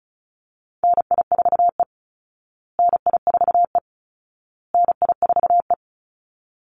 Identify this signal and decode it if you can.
Morse code, heard as a keyed tone, about 34 wpm, 700 Hz. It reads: NI4E NI4E NI4E
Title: Call Signs Repeated 2x - 35wpm